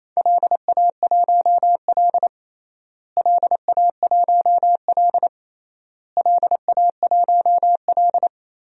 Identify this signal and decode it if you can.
Morse code, heard as a keyed tone, about 28 wpm, 700 Hz. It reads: LA1L LA1L LA1L
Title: DXCC-100 Call Signs - 4 Characters - Repeated 3x - 28wpm